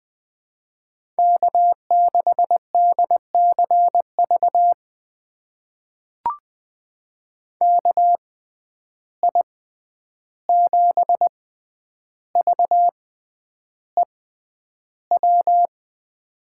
Morse code, heard as a keyed tone, about 20 wpm, spaced as for 10 wpm, 700 Hz. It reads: K6DCV KI7VEW
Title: USA Call Signs - 6x Character Spacing - 20wpm